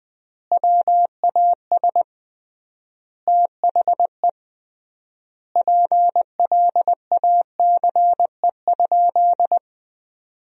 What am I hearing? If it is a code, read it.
Morse code, heard as a keyed tone, about 21 wpm, 700 Hz. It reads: WAS THE PLACE?
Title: 3x Word-Spacing - Sentences from Top 200 Words - 20wpm